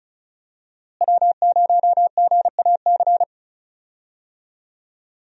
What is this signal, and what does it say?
Morse code, heard as a keyed tone, about 36 wpm, 700 Hz. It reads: W0GAC
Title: USA Call Signs - 5 Characters - Repeated 2x - 35wpm